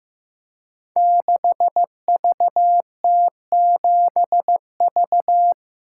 Morse code, heard as a keyed tone, about 15 wpm, 700 Hz. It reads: 6VT7V
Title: Code Groups - 5 Letter:Numbers - 15wpm